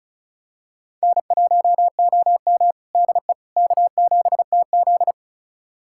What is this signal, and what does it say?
Morse code, heard as a keyed tone, about 36 wpm, 700 Hz. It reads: N1OM DE K7TZ
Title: USA Callsign DE Callsign - 4 Characters - 35wpm